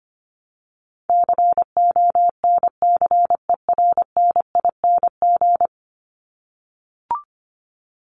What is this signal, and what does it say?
Morse code, heard as a keyed tone, about 25 wpm, 700 Hz. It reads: CONCERNING
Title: Top 1751-2000 Words 25wpm